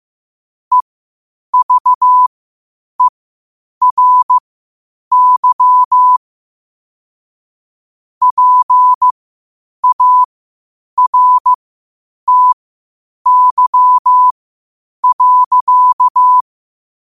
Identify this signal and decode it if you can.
Morse code, heard as a keyed tone, about 15 wpm, spaced as for 8 wpm, 1000 Hz. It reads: EVERY PARTY.
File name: qotdfiles / 8wpm